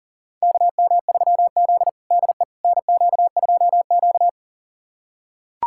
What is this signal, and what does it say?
Morse code, heard as a keyed tone, about 40 wpm, 700 Hz. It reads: KM3Z DE NQ2Q
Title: USA Callsign DE Callsign - 4 Characters - 40wpm